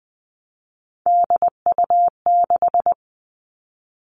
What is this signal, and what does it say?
Morse code, heard as a keyed tone, about 20 wpm, 700 Hz. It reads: DU6
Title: Three Challenging Letter-Number Combinations 20wpm